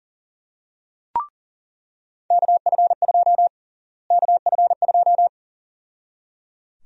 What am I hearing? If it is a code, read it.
Morse code, heard as a keyed tone, about 37 wpm, 700 Hz. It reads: KF2 KF2
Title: USA Call Sign Prefixes - Repeated 2x - 40wpm